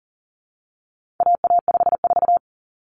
Morse code, heard as a keyed tone, about 40 wpm, 700 Hz. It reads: AA54
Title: Common Cut Numbers - 4-Digit 40wpm